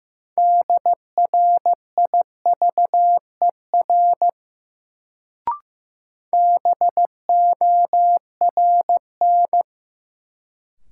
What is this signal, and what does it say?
Morse code, heard as a keyed tone, about 15 wpm, 700 Hz. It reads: DRIVER BORN
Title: Sentences from Top 1000 Words - New Word Drill - 15wpm